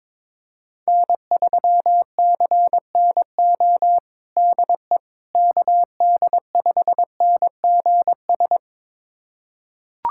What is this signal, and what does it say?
Morse code, heard as a keyed tone, about 22 wpm, 700 Hz. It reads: N3CNO DE KD5NGS
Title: USA Callsign DE Callsign 22wpm